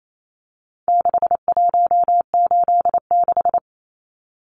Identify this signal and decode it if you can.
Morse code, heard as a keyed tone, about 28 wpm, 700 Hz. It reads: 6186
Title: Numbers - 4-Digit 28wpm